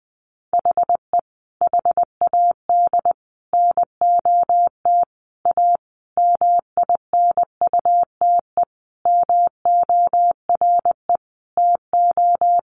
Morse code, heard as a keyed tone, about 20 wpm, 700 Hz. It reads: HE HAD NOT A MINUTE MORE TO
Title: Sentences from Top 1100 Words 20wpm